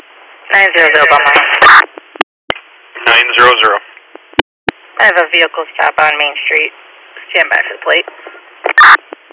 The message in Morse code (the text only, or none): none